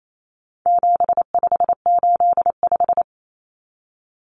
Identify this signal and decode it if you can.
Morse code, heard as a keyed tone, about 28 wpm, 700 Hz. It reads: 7585
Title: Common Cut Numbers - 4-Digit 28wpm